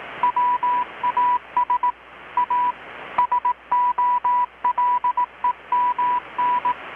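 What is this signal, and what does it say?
Morse code, heard as a keyed tone, about 18 wpm, 980 Hz. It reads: WAS A SOLEMN
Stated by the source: Morse from KPH's Night of Nights XX maritime radiotelegraphy event, received in Chiapas, Mexico about 3,800 km away on a Kenwood R-600 with a grounded 10 m long-wire antenna.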